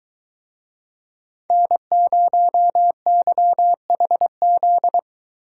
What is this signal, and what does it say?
Morse code, heard as a keyed tone, about 22 wpm, 700 Hz. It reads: N0YHZ